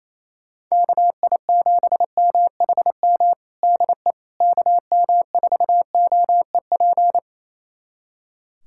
Morse code, heard as a keyed tone, about 28 wpm, 700 Hz. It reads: KI7MHM DE KM4OEP
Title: USA Callsign DE Callsign - 6 Characters - 28wpm